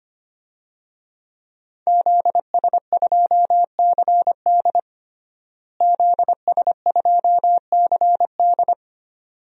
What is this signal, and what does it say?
Morse code, heard as a keyed tone, about 25 wpm, 700 Hz. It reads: ZS2CD ZS2CD
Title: DXCC-100 Call Signs - 5 Characters - Repeated 2x - 25wpm